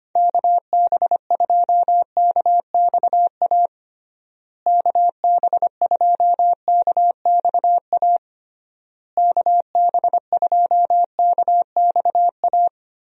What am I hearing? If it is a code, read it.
Morse code, heard as a keyed tone, about 25 wpm, 700 Hz. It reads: KB2KXA KB2KXA KB2KXA
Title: USA Call Signs - 6 Characters - Repeated 3x - 25wpm